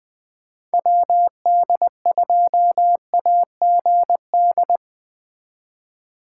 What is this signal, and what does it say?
Morse code, heard as a keyed tone, about 19 wpm, 700 Hz. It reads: WD2AGD
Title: Speed Racing - USA Call Signs 17wpm